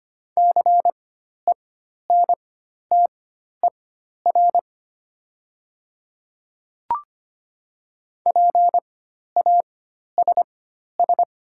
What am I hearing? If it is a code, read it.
Morse code, heard as a keyed tone, about 25 wpm, spaced as for 12 wpm, 700 Hz. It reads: CENTER PASS
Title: 4x Character Spacing - Top 500 Words - 25wpm